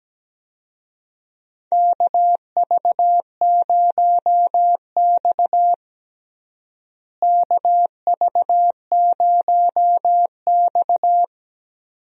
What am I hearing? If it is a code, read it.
Morse code, heard as a keyed tone, about 17 wpm, 700 Hz. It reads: KV0X KV0X
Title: USA Call Signs - 4 Characters - Repeated 2x - 17wpm